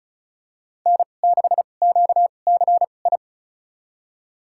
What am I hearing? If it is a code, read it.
Morse code, heard as a keyed tone, about 36 wpm, spaced as for 25 wpm, 700 Hz. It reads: N6QCI
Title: USA Call Signs - 5 Characters - 2x Character Spacing - 35wpm